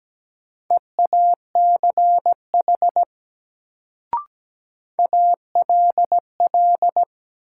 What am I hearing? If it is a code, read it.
Morse code, heard as a keyed tone, about 17 wpm, 700 Hz. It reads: EACH ALL
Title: Sets of 2 Words - New Word Drill - 17wpm